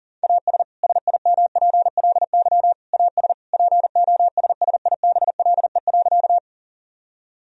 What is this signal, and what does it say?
Morse code, heard as a keyed tone, about 40 wpm, 700 Hz. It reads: AS SIMPLY AS POSSIBLE.
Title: Sentences from Top 1000 Words - Review Part 4 - 40wpm